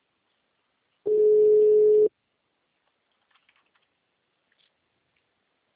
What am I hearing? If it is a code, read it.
Ringback tone (Europe, 425 Hz).